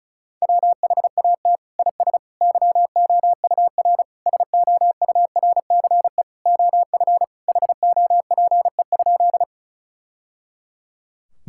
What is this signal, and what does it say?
Morse code, heard as a keyed tone, about 36 wpm, 700 Hz. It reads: WHAT IS YOUR SOURCE OF HOPE?
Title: Sentences from Top 1000 Words - Review Part 5 - 35wpm